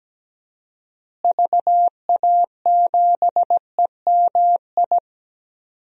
Morse code, heard as a keyed tone, about 17 wpm, 700 Hz. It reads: VA7EMI
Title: DXCC-100 Call Signs - 6 Characters - Repeated 2x - 17wpm